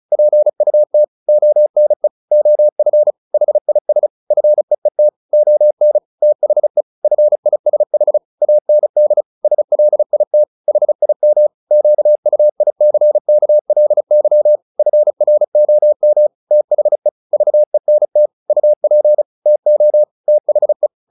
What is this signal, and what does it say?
Morse code, heard as a keyed tone, about 34 wpm, 600 Hz. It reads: PUT ONE OF HIS FEET ON THE FISH AND SLIT HIM QUICKLY FROM THE VENT UP TO THE
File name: the old man and the sea 2 35wpm